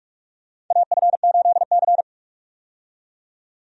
Morse code, heard as a keyed tone, about 43 wpm, 700 Hz. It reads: AF8C
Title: USA Call Signs - 4 Characters - 45wpm